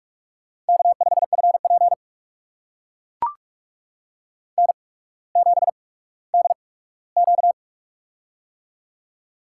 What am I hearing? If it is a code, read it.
Morse code, heard as a keyed tone, about 44 wpm, spaced as for 19 wpm, 700 Hz. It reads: K5FP N7DQ
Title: USA Call Signs - 8x Character Spacing - 45wpm